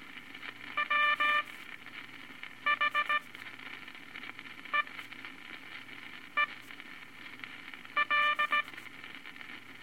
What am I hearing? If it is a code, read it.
Morse code, heard as a keyed tone, about 18 wpm, spaced as for 5 wpm, 1180 Hz, with noise. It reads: WHEEL
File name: Morse Code Records/15A LCM 158 Units I-V Groups of Five